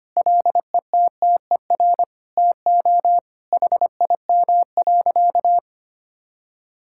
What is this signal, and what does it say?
Morse code, heard as a keyed tone, about 25 wpm, 700 Hz. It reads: LETTER TO HIM.
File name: practice / 25wpm / Sentences with 2-Word Phrases - 25wpm